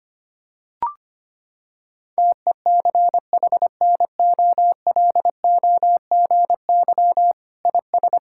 Morse code, heard as a keyed tone, about 25 wpm, 700 Hz. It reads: TECHNOLOGY IS
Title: Sentences from Top 1000 Words - Review Part 5 - 25wpm